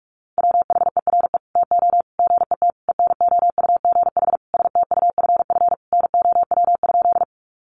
Morse code, heard as a keyed tone, about 44 wpm, 700 Hz. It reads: WHERE TO GET ENOUGH STUFF NOW?
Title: Sentences from Top 1400 Words 45wpm